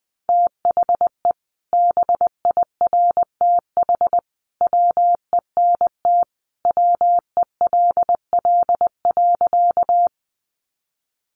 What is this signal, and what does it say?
Morse code, heard as a keyed tone, about 19 wpm, 700 Hz. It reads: THE BIRTH WENT WELL.
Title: Sentences from Top 800 Words 20wpm